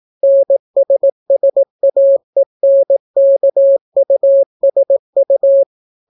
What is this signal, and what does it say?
Morse code, heard as a keyed tone, about 18 wpm, 550 Hz. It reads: NSSAENKUSU